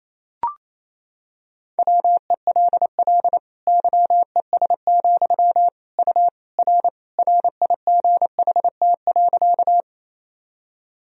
Morse code, heard as a keyed tone, about 28 wpm, 700 Hz. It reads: WELL YES, U R RIGHT.
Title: Sentences from Top 500 Words and Abbreviations 28wpm